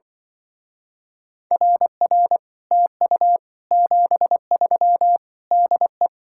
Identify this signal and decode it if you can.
Morse code, heard as a keyed tone, about 24 wpm, 700 Hz. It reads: RR TU 73 DE
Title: Speed-Racing Common QSO Phrases - 20wpm